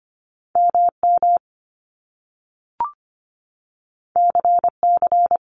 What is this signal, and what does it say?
Morse code, heard as a keyed tone, about 25 wpm, 700 Hz. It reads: MM CC
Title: Top Two Letter Combinations in Words 25wpm